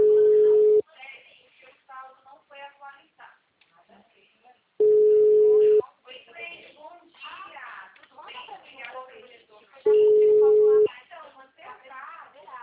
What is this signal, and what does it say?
Ringback tone (Europe, 425 Hz), with speech.
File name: out- -2008- - - .11